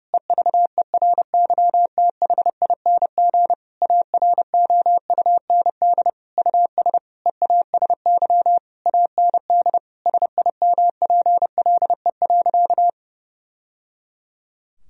Morse code, heard as a keyed tone, about 30 wpm, 700 Hz. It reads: EVERYTHING AROUND US EASY AND SIMPLE.
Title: Sentences from Top 1000 Words - Review Part 3 - 30wpm